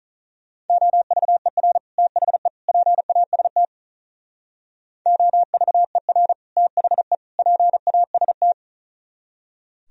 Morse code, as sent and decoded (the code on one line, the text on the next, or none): --- ...- . .-. / - .... / .--. .- ... - / --- ...- . .-. / - .... / .--. .- ... -
OVER THE PAST OVER THE PAST